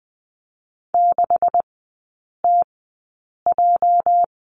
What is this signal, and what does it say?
Morse code, heard as a keyed tone, about 20 wpm, spaced as for 8 wpm, 700 Hz. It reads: 6TJ
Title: 205 Lesson 33e - Letter j - Three Characters 20wpm